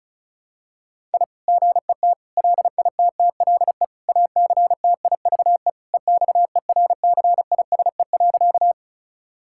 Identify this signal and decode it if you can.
Morse code, heard as a keyed tone, about 36 wpm, 700 Hz. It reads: I GET LITTLE ACTIVE EXERCISE.
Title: Sentences from Top 1000 Words - Review Part 6 - 35wpm